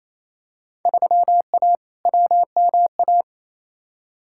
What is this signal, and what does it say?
Morse code, heard as a keyed tone, about 28 wpm, 700 Hz. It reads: 3A WMA
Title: Field Day Exchanges Encore - 28wpm